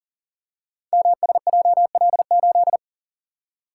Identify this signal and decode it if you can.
Morse code, heard as a keyed tone, about 40 wpm, 690 Hz. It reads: MSJL8
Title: Code Groups - 5 Letter:Numbers - 40wpm